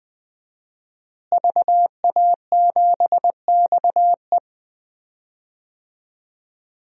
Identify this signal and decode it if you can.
Morse code, heard as a keyed tone, about 20 wpm, 700 Hz. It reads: VA7XE
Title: DXCC-100 Call Signs - 5 Characters - Repeated 3x - 20wpm